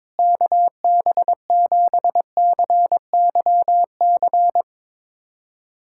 Morse code, heard as a keyed tone, about 22 wpm, 700 Hz. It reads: KB7CYC
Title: USA Callsign DE Callsign - 6 Characters - 22wpm